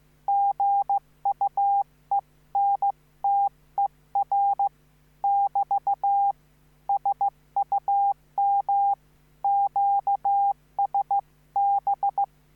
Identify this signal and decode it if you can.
Morse code, heard as a keyed tone, about 15 wpm, 800 Hz. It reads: GUENTER <BT> SUM QSB